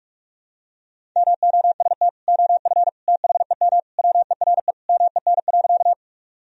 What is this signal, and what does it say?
Morse code, heard as a keyed tone, about 46 wpm, 700 Hz. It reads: MOST OF THEM WERE MEN.
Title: Sentences from Top 1000 Words - Review Part 1 - 45wpm